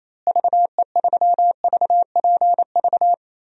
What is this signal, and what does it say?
Morse code, heard as a keyed tone, about 28 wpm, 700 Hz. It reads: VE3VPV
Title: DXCC-100 Call Signs - 6 Characters - Repeated 3x - 28wpm